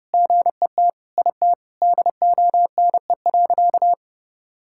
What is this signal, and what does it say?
Morse code, heard as a keyed tone, about 30 wpm, 700 Hz. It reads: GET IT DONE.
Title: Sentences from Top 1000 Words - Review Part 5 - 30wpm